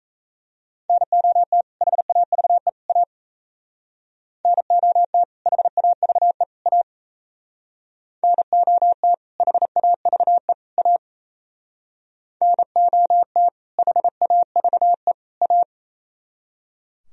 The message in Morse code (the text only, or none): NOT HAVE A NOT HAVE A NOT HAVE A NOT HAVE A